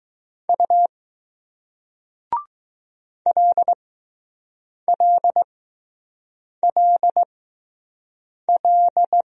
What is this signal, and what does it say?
Morse code, heard as a keyed tone, about 20 wpm, 700 Hz. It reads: U L L L L